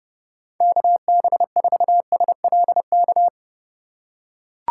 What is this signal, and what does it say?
Morse code, heard as a keyed tone, about 30 wpm, 700 Hz. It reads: KB4SLK
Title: USA Call Signs - 2x Character Spacing - 30wpm